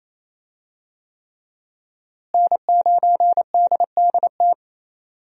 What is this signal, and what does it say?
Morse code, heard as a keyed tone, about 28 wpm, 700 Hz. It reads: N9DDT
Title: USA Call Signs - 5 Characters - Repeated 3x - 28wpm